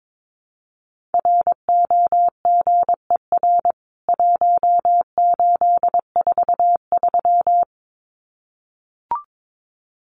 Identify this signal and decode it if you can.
Morse code, heard as a keyed tone, about 22 wpm, 700 Hz. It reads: ROGER 1843
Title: CWT Exchange 22wpm